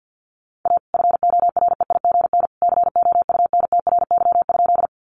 Morse code, heard as a keyed tone, about 48 wpm, 700 Hz. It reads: A FOREIGN COUNTRY?
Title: Sentences from Top 1400 Words 50wpm